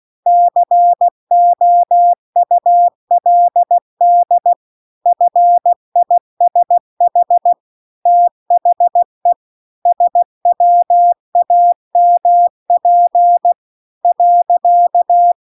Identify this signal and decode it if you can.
Morse code, heard as a keyed tone, about 16 wpm, 700 Hz. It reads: COULD FISH THE SWAMP .
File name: CW-The-Big-Twoheaded-River-2-16WPM